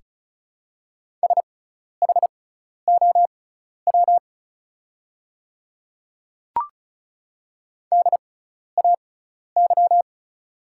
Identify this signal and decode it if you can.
Morse code, heard as a keyed tone, about 36 wpm, spaced as for 12 wpm, 690 Hz. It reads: SHOW DAY
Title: 6x Character Spacing - Top 300 Words - 35wpm